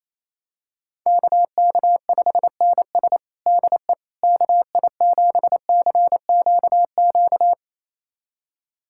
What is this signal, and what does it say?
Morse code, heard as a keyed tone, about 28 wpm, 700 Hz. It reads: KK5NS DE KI7CQQ
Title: USA Callsign DE Callsign 28wpm